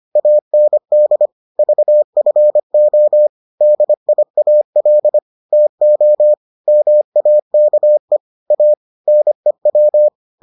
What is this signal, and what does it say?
Morse code, heard as a keyed tone, about 25 wpm, 600 Hz. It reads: AND VFO DIAL TO MAKE A NEW